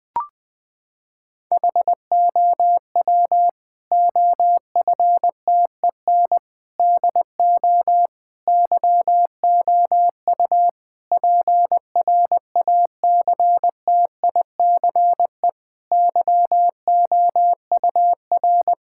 Morse code, heard as a keyed tone, about 21 wpm, 700 Hz. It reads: HOW OFTEN DO YOU PRACTICE YOUR
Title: Sentences from Top 1000 Words - Review Part 6 - 20wpm